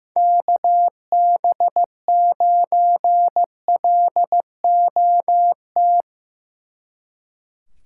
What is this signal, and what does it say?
Morse code, heard as a keyed tone, about 17 wpm, 700 Hz. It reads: KB9LOT
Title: Speed Racing - USA Call Signs 15wpm